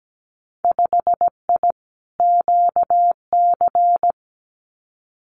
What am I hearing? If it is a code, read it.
Morse code, heard as a keyed tone, about 17 wpm, 700 Hz. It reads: 5I QC
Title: Winter Field Day Exchanges Encore 17wpm